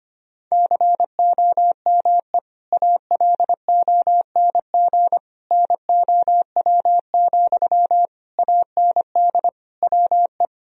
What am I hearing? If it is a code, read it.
Morse code, heard as a keyed tone, about 25 wpm, 690 Hz. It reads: COME ALONG NOW, AND WE
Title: Sentences from Top 1000 Words - Review Part 4 - 25wpm